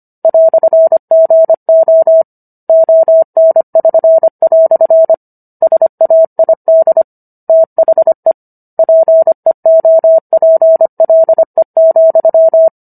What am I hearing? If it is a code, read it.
Morse code, heard as a keyed tone, about 25 wpm, 650 Hz. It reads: "GO ON<SN>" SAID THE PEOPLE,